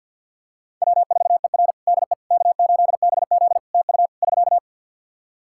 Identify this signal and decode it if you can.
Morse code, heard as a keyed tone, about 50 wpm, 700 Hz. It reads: W4ER DE K8BZ TU <SK>
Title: Practice QSO - Exchange 4 - Ending - 50wpm